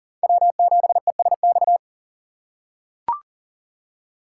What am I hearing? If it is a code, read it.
Morse code, heard as a keyed tone, about 40 wpm, 700 Hz. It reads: W7ESX
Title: USA Call Signs - 5 Characters - 2x Character Spacing - 40wpm